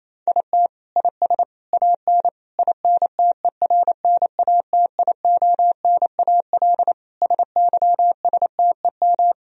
Morse code, heard as a keyed tone, about 28 wpm, 700 Hz. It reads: IT IS AN INTERNATIONAL SYSTEM
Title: Sentences from Top 1000 Words - Review Part 4 - 28wpm